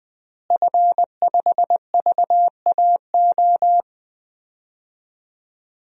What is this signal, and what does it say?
Morse code, heard as a keyed tone, about 20 wpm, 700 Hz. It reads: F5VAO
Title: DXCC-100 Call Signs - 5 Characters - Repeated 3x - 20wpm